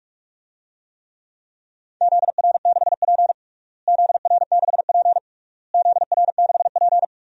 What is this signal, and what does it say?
Morse code, heard as a keyed tone, about 41 wpm, 700 Hz. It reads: ZR6P ZR6P ZR6P
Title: DXCC-100 Call Signs - 4 Characters - Repeated 3x - 45wpm